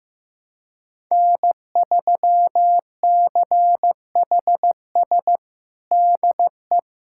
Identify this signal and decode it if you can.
Morse code, heard as a keyed tone, about 15 wpm, 700 Hz. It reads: N3CHS DE
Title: USA Callsign DE Callsign - 5 Characters - 15wpm